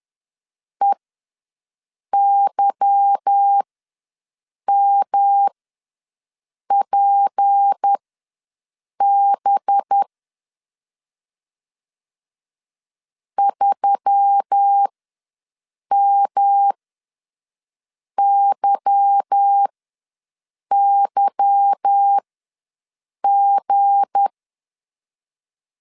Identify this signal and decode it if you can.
Morse code, heard as a keyed tone, about 10 wpm, spaced as for 6 wpm, 800 Hz. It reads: EYMPB 3MYYG